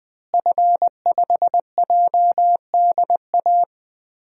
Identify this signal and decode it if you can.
Morse code, heard as a keyed tone, about 20 wpm, 700 Hz. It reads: F5JDA
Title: DXCC-100 Call Signs - 5 Characters - Repeated 3x - 20wpm